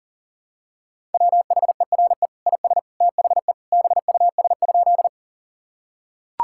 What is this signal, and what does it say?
Morse code, heard as a keyed tone, about 40 wpm, 700 Hz. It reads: WHERE IS THE BUS?
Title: Sentences from Top 1000 Words - Review Part 6 - 40wpm